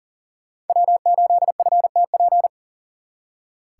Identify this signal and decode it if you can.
Morse code, heard as a keyed tone, about 52 wpm, 700 Hz. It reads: W8FTP